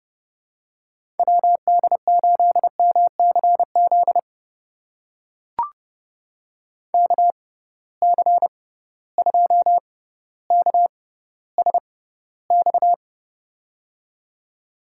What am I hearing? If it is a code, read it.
Morse code, heard as a keyed tone, about 31 wpm, spaced as for 15 wpm, 700 Hz. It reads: WD8MCZ KC2KSX
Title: USA Call Signs - 6 Characters - 6x Character Spacing - 30wpm